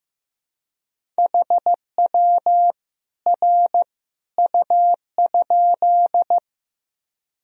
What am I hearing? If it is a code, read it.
Morse code, heard as a keyed tone, about 15 wpm, 700 Hz. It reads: HW R U?
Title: Sentences from Top 100 Words and Abbreviations 15wpm